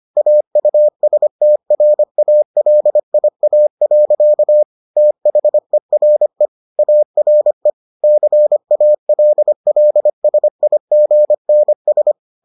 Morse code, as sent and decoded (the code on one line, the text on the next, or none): .- ..- ... - .-. .- .-.. .. .- .-.-.- / - .... .-. / .- .-. / -.-. .- .-.. .-.. ... .. --. -. ...
AUSTRALIA. THERE ARE CALLSIGNS